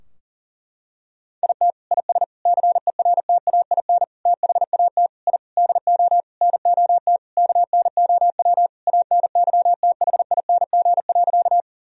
Morse code, heard as a keyed tone, about 37 wpm, 700 Hz. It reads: IT IS CERTAIN THAT I DO NOT KNOW ANYTHING.
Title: Sentences from Top 1000 Words - Review Part 2 - 40wpm